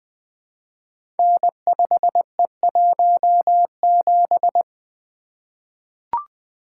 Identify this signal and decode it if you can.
Morse code, heard as a keyed tone, about 20 wpm, 700 Hz. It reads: N5E17
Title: Code Groups - 5 Letter:Numbers - 20wpm